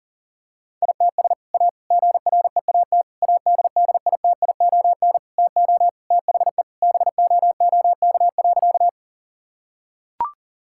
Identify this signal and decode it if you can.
Morse code, heard as a keyed tone, about 40 wpm, 700 Hz. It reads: ITS A GREAT ADDITION TO THE BOOK.
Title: Sentences from Top 1000 Words - Review Part 6 - 40wpm